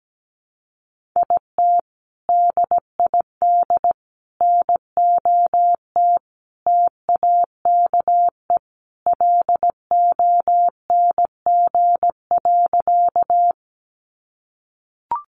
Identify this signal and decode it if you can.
Morse code, heard as a keyed tone, about 17 wpm, 700 Hz. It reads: IT DID NOT TAKE LONG.
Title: Sentences from Top 600 Words 17wpm